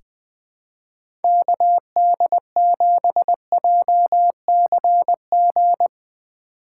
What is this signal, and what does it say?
Morse code, heard as a keyed tone, about 20 wpm, 700 Hz. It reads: KD7JCG
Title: USA Call Signs - 6 Characters - 8x Character Spacing - 20wpm